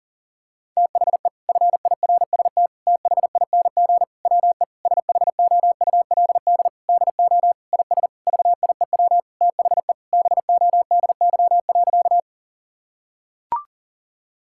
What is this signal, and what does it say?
Morse code, heard as a keyed tone, about 40 wpm, 700 Hz. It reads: THE FIRST THING WE SHOULD DO IS VIEW THE BODY.
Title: Sentences from Top 1000 Words - Review Part 6 - 40wpm